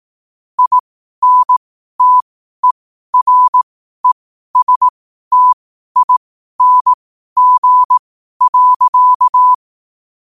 Morse code, heard as a keyed tone, about 18 wpm, spaced as for 12 wpm, 1000 Hz. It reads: INTERESTING.
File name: qotdfiles / 12wpm